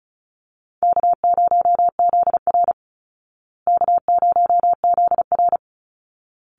Morse code, heard as a keyed tone, about 34 wpm, 700 Hz. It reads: K0ZR K0ZR
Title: Easy Call Signs Repeated 2x - 35wpm